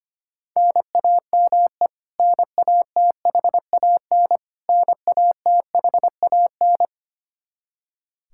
Morse code, heard as a keyed tone, about 25 wpm, 700 Hz. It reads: NAME NATHAN NATHAN
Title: Practice QSO - Exchange 1 - Beginning - 25wpm